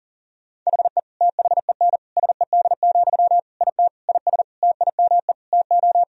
Morse code, heard as a keyed tone, about 40 wpm, 700 Hz. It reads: HE THEN SED, IT IS TIME TO
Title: Sentences from Top 300 Words and Abbreviations 40wpm